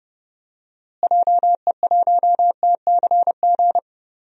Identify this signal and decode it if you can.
Morse code, heard as a keyed tone, about 30 wpm, 700 Hz. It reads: JE1TCG
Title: DXCC-100 Call Signs - 6 Characters - Repeated 3x - 30wpm